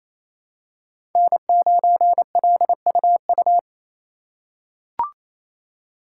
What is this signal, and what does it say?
Morse code, heard as a keyed tone, about 28 wpm, 700 Hz. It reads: N9LUU
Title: USA Call Signs - 5 Characters - 2x Character Spacing - 28wpm